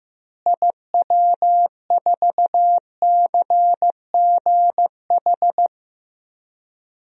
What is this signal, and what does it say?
Morse code, heard as a keyed tone, about 15 wpm, 700 Hz. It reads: IW4CGH
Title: DXCC-100 Call Signs - 6 Characters - Repeated 3x - 15wpm